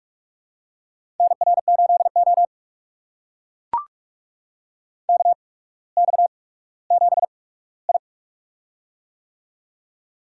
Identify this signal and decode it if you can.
Morse code, heard as a keyed tone, about 44 wpm, spaced as for 19 wpm, 700 Hz. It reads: NR8O KX7I
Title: USA Call Signs - 4 Characters - 8x Character Spacing - 45wpm